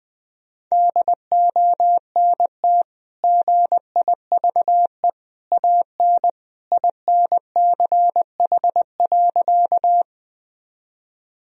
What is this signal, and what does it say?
Morse code, heard as a keyed tone, about 20 wpm, 700 Hz. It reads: DONT GIVE AN INCH.
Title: Sentences from Top 1000 Words - Review Part 3 - 20wpm